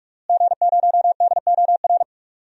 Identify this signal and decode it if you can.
Morse code, heard as a keyed tone, about 46 wpm, 700 Hz. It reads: G0DOR
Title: DXCC-100 Call Signs - 5 Characters - Repeated 2x - 45wpm